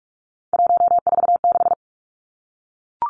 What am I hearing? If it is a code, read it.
Morse code, heard as a keyed tone, about 44 wpm, 700 Hz. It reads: 146
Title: Numbers - 3-Digit 45wpm